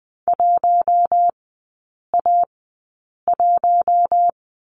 Morse code, heard as a keyed tone, about 20 wpm, spaced as for 8 wpm, 700 Hz. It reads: 1A1
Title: 047 Lesson 9e - Number 4 - Three Characters 20wpm